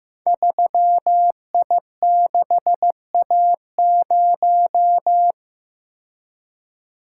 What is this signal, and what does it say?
Morse code, heard as a keyed tone, about 15 wpm, 700 Hz. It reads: 3I6A0
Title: Code Groups - 5 Letter:Numbers - 15wpm